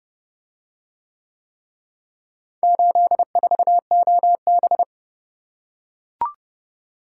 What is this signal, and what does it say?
Morse code, heard as a keyed tone, about 29 wpm, 700 Hz. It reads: 84OB